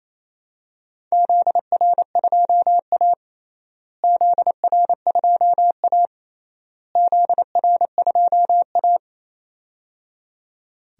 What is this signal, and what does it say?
Morse code, heard as a keyed tone, about 28 wpm, 700 Hz. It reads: ZR2A ZR2A ZR2A
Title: DXCC-100 Call Signs - 4 Characters - Repeated 3x - 28wpm